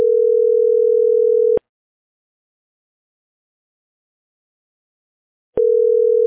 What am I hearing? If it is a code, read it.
Ringback tone (North America), 440 + 480 Hz.